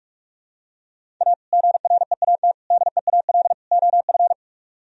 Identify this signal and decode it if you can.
Morse code, heard as a keyed tone, about 44 wpm, 700 Hz. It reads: A GREAT DEAL OF